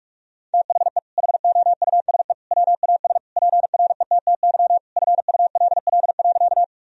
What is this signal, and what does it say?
Morse code, heard as a keyed tone, about 46 wpm, 700 Hz. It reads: THE HOUSE WAS PRETTY FULL.
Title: Sentences from Top 1000 Words - Review Part 3 - 45wpm